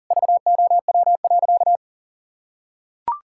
Morse code, heard as a keyed tone, about 40 wpm, 700 Hz. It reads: VOW.
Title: Pangram Challenge v2 - 40wpm